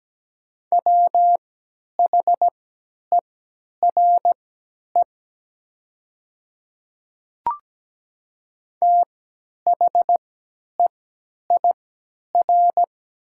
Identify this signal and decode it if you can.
Morse code, heard as a keyed tone, about 17 wpm, spaced as for 10 wpm, 700 Hz. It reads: WHERE THEIR
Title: 3x Character Spacing - Top 300 Words - 17wpm